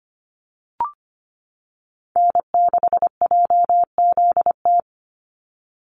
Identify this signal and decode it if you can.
Morse code, heard as a keyed tone, about 25 wpm, 700 Hz. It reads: N6JZT
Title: NA SOTA Activator Callsigns 25wpm